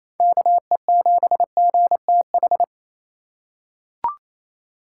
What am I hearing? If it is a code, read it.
Morse code, heard as a keyed tone, about 28 wpm, 690 Hz. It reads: KE7GTH
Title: USA Call Signs - 6 Characters - 4x Character Spacing - 28wpm